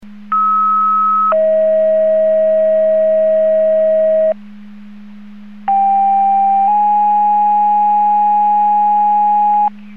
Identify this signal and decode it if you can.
Two-tone page, heard as a one-second tone, then a three-second tone, about 1290 then 640 Hz, about 790 then 830 Hz. It is heard from 0.3 to 9.7 s.